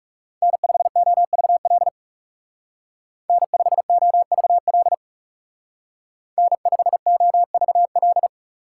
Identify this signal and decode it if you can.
Morse code, heard as a keyed tone, about 39 wpm, 700 Hz. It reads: N5OVL N5OVL N5OVL